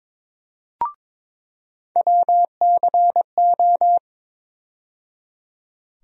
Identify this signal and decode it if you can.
Morse code, heard as a keyed tone, about 21 wpm, 700 Hz. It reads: WCO
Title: USA Call Sign Postfixes - 3 Letters 22wpm